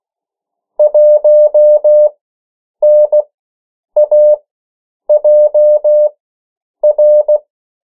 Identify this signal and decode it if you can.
Morse code, heard as a keyed tone, about 16 wpm, spaced as for 8 wpm, 600 Hz. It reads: 1NAJR
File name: morse mp3 / 14 urok 13